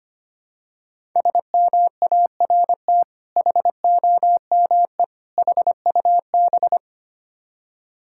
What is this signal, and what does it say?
Morse code, heard as a keyed tone, about 25 wpm, 700 Hz. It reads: SMART HOME HUB
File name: practice / 25wpm / Difficult Words with Hints - 25wpm